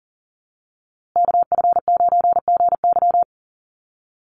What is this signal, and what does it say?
Morse code, heard as a keyed tone, about 40 wpm, 700 Hz. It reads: KF9GY
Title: USA Call Signs Repeated 2x 40wpm